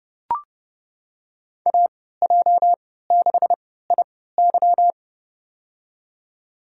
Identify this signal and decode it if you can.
Morse code, heard as a keyed tone, about 30 wpm, spaced as for 17 wpm, 700 Hz. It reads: AJ6IY
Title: USA Call Signs - 5 Characters - 3x Character Spacing - 30wpm